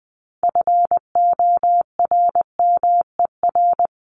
Morse code, heard as a keyed tone, about 20 wpm, 700 Hz. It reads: FORMER
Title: LY Suffix - Top 100 Words with Base Words 20wpm